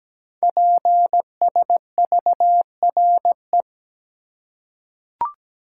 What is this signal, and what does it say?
Morse code, heard as a keyed tone, about 17 wpm, 700 Hz. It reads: PSVRE
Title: Code Groups - 5 Letter:Numbers - 17wpm